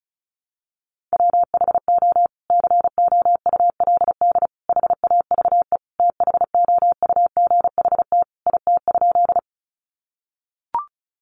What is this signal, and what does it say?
Morse code, heard as a keyed tone, about 34 wpm, 700 Hz. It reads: WHO COULD HAVE THOUGHT IT?
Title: Sentences from Top 300 Words 35wpm